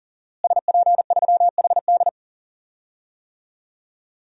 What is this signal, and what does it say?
Morse code, heard as a keyed tone, about 37 wpm, 700 Hz. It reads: SP3HD
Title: DXCC-100 Call Signs - 5 Characters - Repeated 3x - 40wpm